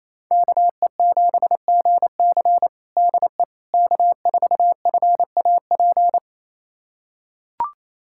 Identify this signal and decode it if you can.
Morse code, heard as a keyed tone, about 28 wpm, 700 Hz. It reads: KE7GC DE K4FAP
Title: USA Callsign DE Callsign 28wpm